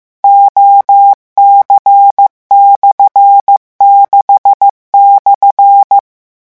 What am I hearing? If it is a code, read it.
Morse code, heard as a keyed tone, about 15 wpm, 800 Hz. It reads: OC/6/
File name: morse mp3 / revision12-vitesse 14 mots